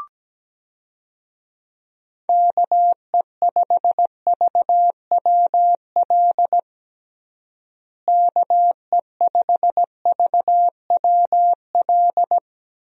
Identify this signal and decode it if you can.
Morse code, heard as a keyed tone, about 17 wpm, 700 Hz. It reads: KE5VWL KE5VWL
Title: USA Call Signs - 6 Characters - Repeated 2x - 17wpm